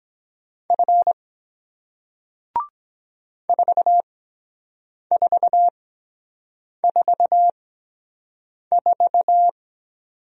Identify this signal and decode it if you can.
Morse code, heard as a keyed tone, about 23 wpm, spaced as for 7 wpm, 700 Hz. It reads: F 4444